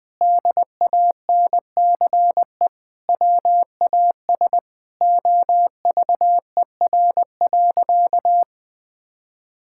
Morse code, heard as a keyed tone, about 20 wpm, 700 Hz. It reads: DANCE WAS OVER.
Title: Sentences from Top 1000 Words - Review Part 6 - 20wpm